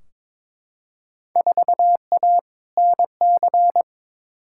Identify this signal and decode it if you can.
Morse code, heard as a keyed tone, about 22 wpm, 700 Hz. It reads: 4A NC
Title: Field Day Exchanges - 22wpm